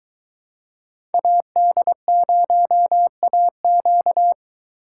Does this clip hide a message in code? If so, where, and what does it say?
Yes: Morse code from 1.1 to 4.3 s, heard as a keyed tone, about 23 wpm, 700 Hz. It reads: AD0AQ